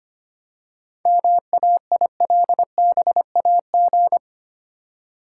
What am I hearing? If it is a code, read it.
Morse code, heard as a keyed tone, about 25 wpm, 690 Hz. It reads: MAILBAG
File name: practice / 25wpm / Easy Words with Hints - 25wpm